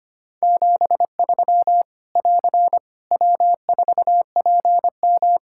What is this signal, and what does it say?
Morse code, heard as a keyed tone, about 25 wpm, 700 Hz. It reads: 73 <AR> W4PM
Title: K1USN SST QSO Practice 25wpm